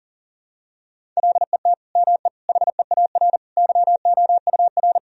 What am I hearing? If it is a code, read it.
Morse code, heard as a keyed tone, about 40 wpm, 700 Hz. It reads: LET ME HEAR YOUR
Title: Sentences from Top 1000 Words - Review Part 1 - 40wpm